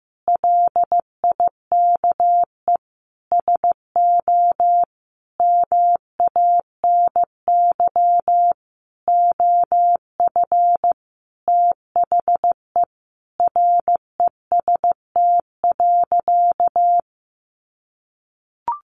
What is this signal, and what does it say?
Morse code, heard as a keyed tone, about 15 wpm, 700 Hz. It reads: LIKE SO MANY OF THE REST.
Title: Sentences from Top 1500 Words 15wpm